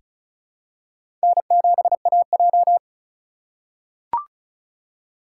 Morse code, heard as a keyed tone, about 35 wpm, 700 Hz. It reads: N7AJ
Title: USA Call Signs - 4 Characters - 4x Character Spacing - 35wpm